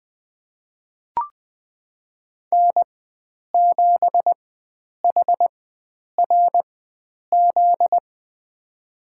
Morse code, heard as a keyed tone, about 20 wpm, spaced as for 9 wpm, 700 Hz. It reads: N7HRZ
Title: USA Call Signs - 5 Characters - 4x Character Spacing - 20wpm